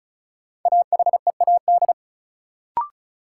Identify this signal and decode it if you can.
Morse code, heard as a keyed tone, about 34 wpm, 700 Hz. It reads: AHEAD
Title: Sentences from Top 1100 Words - New Word Drill Alt - 35wpm